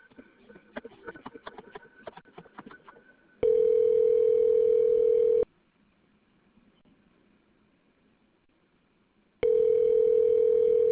Ringback tone (North America), 440 + 480 Hz.